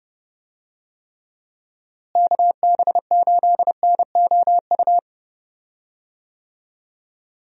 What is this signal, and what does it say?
Morse code, heard as a keyed tone, about 30 wpm, 700 Hz. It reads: KB8NOU
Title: USA Call Signs - 6 Characters - Repeated 2x - 30wpm